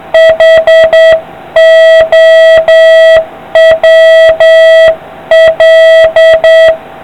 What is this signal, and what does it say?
Morse code, heard as a keyed tone, about 8 wpm, 650 Hz, with noise. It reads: HOWL